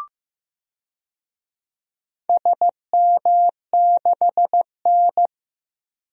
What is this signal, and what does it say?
Morse code, heard as a keyed tone, about 15 wpm, 700 Hz. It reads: SM6N
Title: DXCC-100 Call Signs - 4 Characters - Repeated 3x - 15wpm